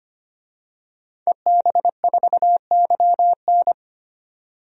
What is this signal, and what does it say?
Morse code, heard as a keyed tone, about 25 wpm, 700 Hz. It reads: EB4YN
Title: DXCC-100 Call Signs - 5 Characters - 25wpm